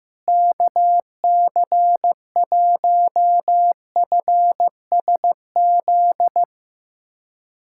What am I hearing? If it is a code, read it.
Morse code, heard as a keyed tone, about 15 wpm, 700 Hz. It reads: KC1FSZ
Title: K1USN SST QSO Practice - 15wpm